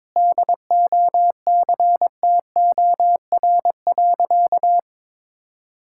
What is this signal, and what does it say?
Morse code, heard as a keyed tone, about 22 wpm, 700 Hz. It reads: DOCTOR.
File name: practice / 22wpm / Sentences with 3-Word Phrases - 22wpm